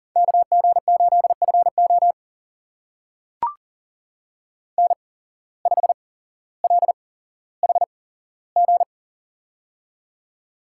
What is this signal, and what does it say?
Morse code, heard as a keyed tone, about 39 wpm, spaced as for 17 wpm, 700 Hz. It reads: KG8FO N5LHG
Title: USA Call Signs - 5 Characters - 8x Character Spacing - 40wpm